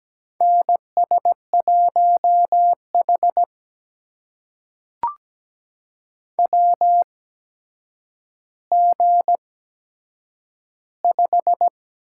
Morse code, heard as a keyed tone, about 17 wpm, spaced as for 8 wpm, 700 Hz. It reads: NS1H WG5
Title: USA Call Signs - 4 Characters - 8x Character Spacing - 17wpm